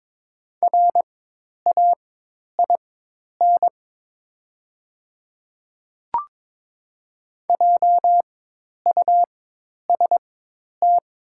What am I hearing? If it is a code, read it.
Morse code, heard as a keyed tone, about 22 wpm, spaced as for 10 wpm, 700 Hz. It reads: RAIN JUST
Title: 4x Character Spacing - Top 300 Words - 22wpm